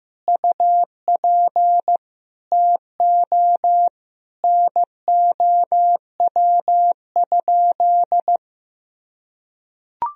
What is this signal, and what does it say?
Morse code, heard as a keyed tone, about 15 wpm, 700 Hz. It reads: UP TO NOW?
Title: Sentences from Top 100 Words and Abbreviations 15wpm